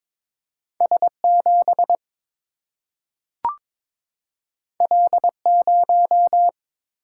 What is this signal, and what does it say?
Morse code, heard as a keyed tone, about 21 wpm, 700 Hz. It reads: S7 L0
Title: Single-Letter Single-Number 22wpm